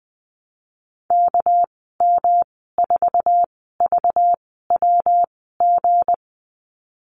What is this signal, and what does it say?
Morse code, heard as a keyed tone, about 20 wpm, spaced as for 15 wpm, 700 Hz. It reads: KM4VWG
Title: 207 Lesson 33g - Letter j - Call Signs 20wpm